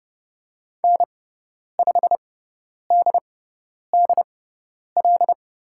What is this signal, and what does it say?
Morse code, heard as a keyed tone, about 30 wpm, spaced as for 10 wpm, 700 Hz. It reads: N5DDL
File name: Lesson 17g - Letter u - Call Signs 30wpm